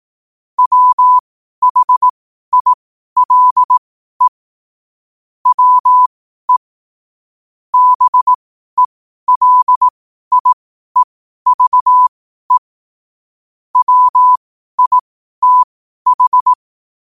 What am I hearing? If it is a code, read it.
Morse code, heard as a keyed tone, about 18 wpm, spaced as for 12 wpm, 1000 Hz. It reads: WHILE WE BELIEVE WITH
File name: qotdfiles / 12wpm